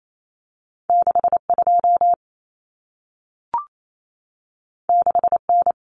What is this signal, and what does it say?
Morse code, heard as a keyed tone, about 28 wpm, 700 Hz. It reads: 62 6N
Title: Common Cut Numbers - 2-Digit 28wpm